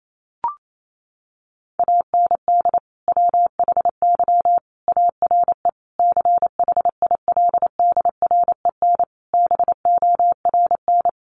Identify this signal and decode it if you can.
Morse code, heard as a keyed tone, about 28 wpm, 700 Hz. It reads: AND WHY ARE CHILDREN BORN
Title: Sentences from Top 1000 Words 28wpm